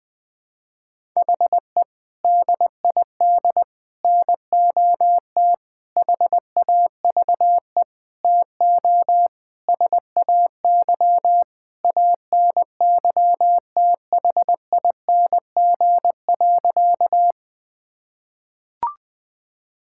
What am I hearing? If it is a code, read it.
Morse code, heard as a keyed tone, about 20 wpm, 700 Hz. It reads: HE DID NOT HAVE TO SAY ANYTHING.
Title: Sentences with 5-Word Phrases - 20wpm